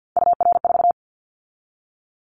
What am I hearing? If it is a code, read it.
Morse code, heard as a keyed tone, about 50 wpm, 700 Hz. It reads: UR4
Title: Callsign Prefixes for Top 25 DXCC - 50wpm